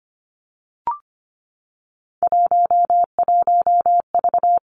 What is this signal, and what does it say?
Morse code, heard as a keyed tone, about 25 wpm, 700 Hz. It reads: 11V
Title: Three Challenging Letter-Number Combinations 25wpm